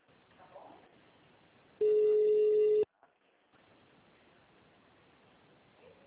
Ringback tone (Europe, 425 Hz).
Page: Jacarepagua - RJ